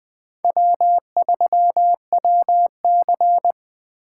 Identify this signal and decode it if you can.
Morse code, heard as a keyed tone, about 19 wpm, 700 Hz. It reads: W3WC